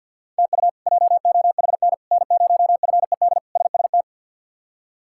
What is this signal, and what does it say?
Morse code, heard as a keyed tone, about 49 wpm, 700 Hz. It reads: TU JOHN N0FED SST
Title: K1USN SST QSO Practice - 50wpm